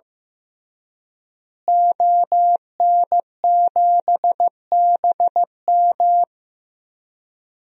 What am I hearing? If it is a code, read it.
Morse code, heard as a keyed tone, about 15 wpm, 700 Hz. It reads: ON7BM
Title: DXCC-100 Call Signs - 5 Characters - Repeated 3x - 15wpm